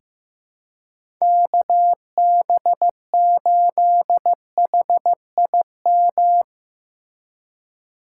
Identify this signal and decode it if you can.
Morse code, heard as a keyed tone, about 15 wpm, 700 Hz. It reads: KB8HIM
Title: USA Call Signs - 6 Characters - 2x Character Spacing - 15wpm